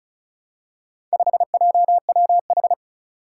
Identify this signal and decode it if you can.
Morse code, heard as a keyed tone, about 36 wpm, 700 Hz. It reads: 5JWH
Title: Four Challenging Letter-Number Combinations 35wpm